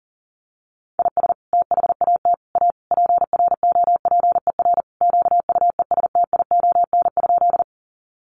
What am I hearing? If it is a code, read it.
Morse code, heard as a keyed tone, about 40 wpm, 700 Hz. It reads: IS THAT A PROPER QUESTION?
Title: Sentences from Top 1100 Words 40wpm